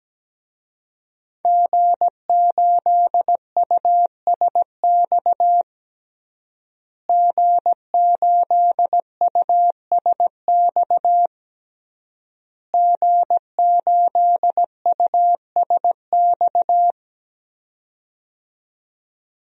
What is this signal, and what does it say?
Morse code, heard as a keyed tone, about 17 wpm, 700 Hz. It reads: G8USX G8USX G8USX
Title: DXCC-100 Call Signs - 5 Characters - Repeated 3x - 17wpm